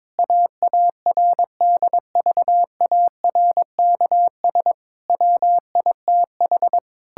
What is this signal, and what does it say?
Morse code, heard as a keyed tone, about 22 wpm, 700 Hz. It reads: AARDVARKS WITH
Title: Pangram Challenge v2 - 22wpm